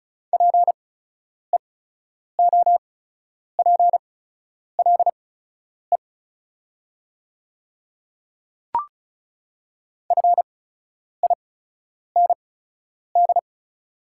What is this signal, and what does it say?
Morse code, heard as a keyed tone, about 36 wpm, spaced as for 10 wpm, 690 Hz. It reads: PEOPLE FIND
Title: 8x Character Spacing - Top 100 Words - 35wpm